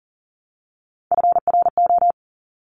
Morse code, heard as a keyed tone, about 42 wpm, 700 Hz. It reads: FRO
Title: Top Three Letter Combinations in Words 40wpm